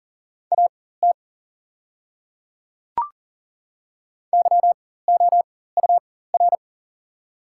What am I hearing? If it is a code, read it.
Morse code, heard as a keyed tone, about 40 wpm, spaced as for 19 wpm, 700 Hz. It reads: AT YOUR
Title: 4x Character Spacing - Top 200 Words - 40wpm